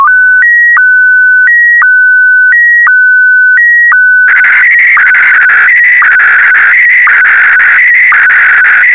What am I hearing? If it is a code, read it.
SSTV (slow-scan television), Scottie DX.